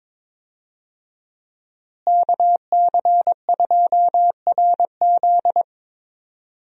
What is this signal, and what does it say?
Morse code, heard as a keyed tone, about 22 wpm, 700 Hz. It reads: KC2RZ
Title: USA Call Signs - 5 Characters - Repeated 2x - 22wpm